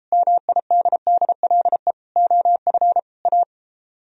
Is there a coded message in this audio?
Yes: Morse code from 0.1 to 3.4 s, heard as a keyed tone, about 33 wpm, 700 Hz. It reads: MIDDLE OF A